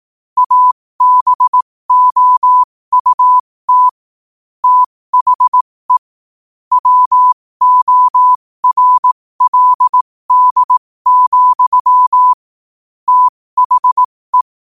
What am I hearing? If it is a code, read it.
Morse code, heard as a keyed tone, about 18 wpm, spaced as for 15 wpm, 1000 Hz. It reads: ABOUT THE WORLD, THE